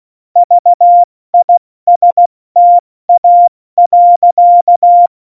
Morse code, heard as a keyed tone, about 16 wpm, 700 Hz. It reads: VISTA.
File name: Morsekurs / Witze / Witz 15. Januar 2019, 70 F80